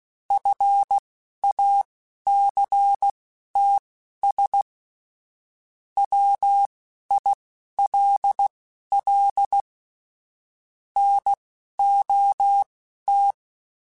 Morse code, heard as a keyed tone, about 16 wpm, spaced as for 12 wpm, 800 Hz. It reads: FACTS WILL NOT